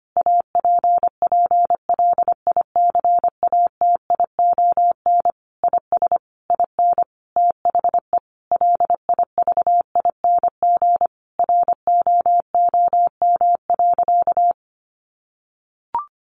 Morse code, heard as a keyed tone, about 25 wpm, 700 Hz. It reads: APPLICATION IS IN THE LIVING ROOM.
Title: Sentences from Top 1200 Words 25wpm